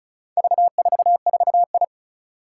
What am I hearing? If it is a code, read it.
Morse code, heard as a keyed tone, about 34 wpm, 690 Hz. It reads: V44I
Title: Four Challenging Letter-Number Combinations 35wpm